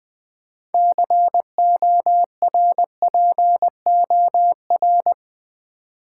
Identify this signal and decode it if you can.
Morse code, heard as a keyed tone, about 20 wpm, 700 Hz. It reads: CORPOR
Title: Build A Word - 3 Letters At A Time - 9-Letter Words - 20wpm